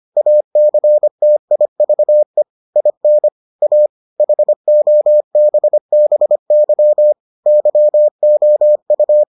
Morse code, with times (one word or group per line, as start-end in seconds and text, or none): 0.2-2.4 ACTIVE
2.8-3.3 IN
3.6-3.9 A
4.2-7.1 HOBBY
7.5-9.2 YOU